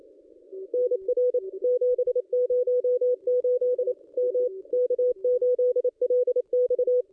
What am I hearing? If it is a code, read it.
Morse code, heard as a keyed tone, about 28 wpm, 500 Hz. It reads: NR 708 M K8LX
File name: contests / ss-cw-k5tr-op-n5rz -1400